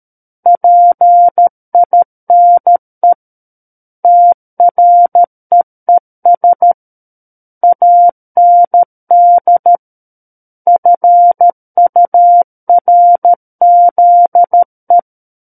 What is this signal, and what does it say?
Morse code, heard as a keyed tone, about 13 wpm, 700 Hz. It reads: PINE TREES AND FURZE
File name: Book One / 04-Chapter 4 - 13 WPM